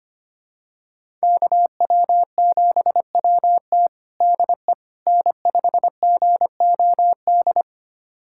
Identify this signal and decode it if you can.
Morse code, heard as a keyed tone, about 25 wpm, 700 Hz. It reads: KW7WT DE N5GOD
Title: USA Callsign DE Callsign - 5 Characters - 25wpm